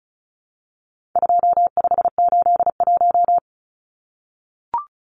Morse code, heard as a keyed tone, about 34 wpm, 700 Hz. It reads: 2581
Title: Numbers - 4-Digit 35wpm